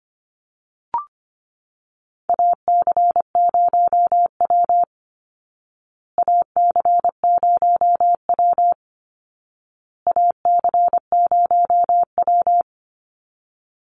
Morse code, heard as a keyed tone, about 25 wpm, 700 Hz. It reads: AC0W AC0W AC0W
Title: Call Signs Repeated 2x - 25wpm